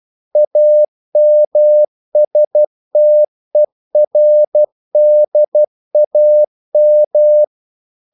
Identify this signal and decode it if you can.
Morse code, heard as a keyed tone, about 12 wpm, 600 Hz. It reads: AMSTERDAM
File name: CW / Practice / NationalCapitals12